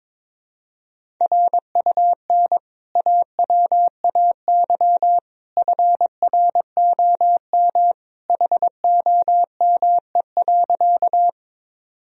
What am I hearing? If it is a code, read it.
Morse code, heard as a keyed tone, about 21 wpm, 700 Hz. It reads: RUN AWAY FROM HOME.
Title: Sentences from Top 1000 Words - Review Part 1 - 22wpm